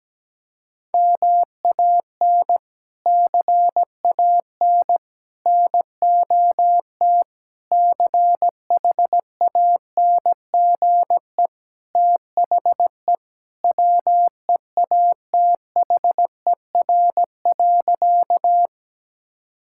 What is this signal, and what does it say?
Morse code, heard as a keyed tone, about 17 wpm, 700 Hz. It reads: MAN CAN NOT CHANGE THE WEATHER.
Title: Sentences from Top 1000 Words - Review Part 5 - 17wpm